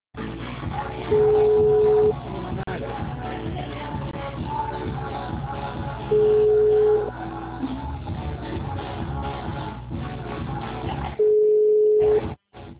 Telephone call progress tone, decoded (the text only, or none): ringback tone